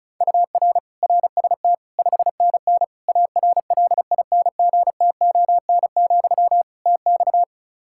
Morse code, heard as a keyed tone, about 34 wpm, 700 Hz. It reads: UR RST 5NN ARLINGTON, TX
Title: Practice QSO - Exchange 1 - Beginning - 35wpm